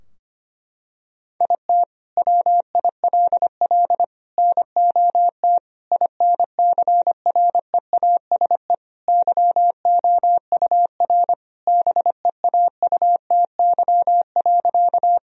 Morse code, heard as a keyed tone, about 25 wpm, 700 Hz. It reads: IT WILL NOT INCREASE YOUR BEAUTY.
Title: Sentences from Top 1000 Words - Review Part 4 - 25wpm